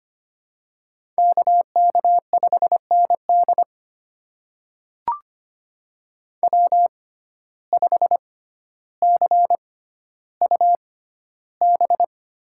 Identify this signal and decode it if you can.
Morse code, heard as a keyed tone, about 25 wpm, spaced as for 13 wpm, 700 Hz. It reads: KK5ND W5CUB
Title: USA Call Signs - 5 Characters - 6x Character Spacing - 25wpm